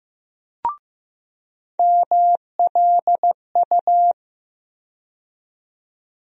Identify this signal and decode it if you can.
Morse code, heard as a keyed tone, about 15 wpm, 700 Hz. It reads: MLU
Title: USA Call Sign Postfixes - 3 Letters 15wpm